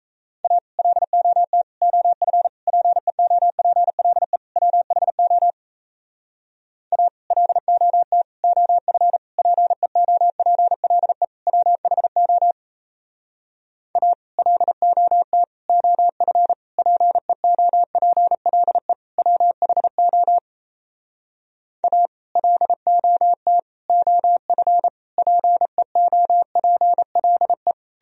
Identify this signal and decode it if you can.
Morse code, heard as a keyed tone, about 38 wpm, 700 Hz. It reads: A LOT OF PEOPLE WHO A LOT OF PEOPLE WHO A LOT OF PEOPLE WHO A LOT OF PEOPLE